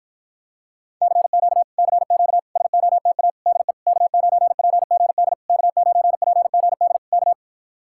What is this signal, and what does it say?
Morse code, heard as a keyed tone, about 51 wpm, 700 Hz. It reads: CQ CQ SOTA DE K9PGD K9PGD K